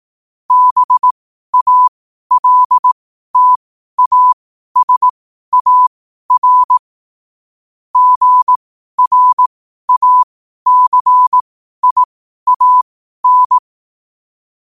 Morse code, heard as a keyed tone, about 18 wpm, spaced as for 12 wpm, 1000 Hz. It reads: BALTASAR GRACIAN
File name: qotdfiles / 12wpm